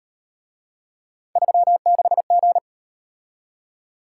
Morse code, heard as a keyed tone, about 38 wpm, 700 Hz. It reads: 36G